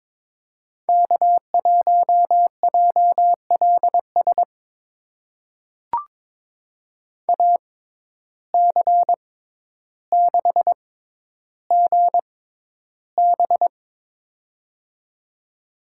Morse code, heard as a keyed tone, about 22 wpm, spaced as for 11 wpm, 700 Hz. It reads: K1JLS AC6GB
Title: USA Call Signs - 5 Characters - 6x Character Spacing - 22wpm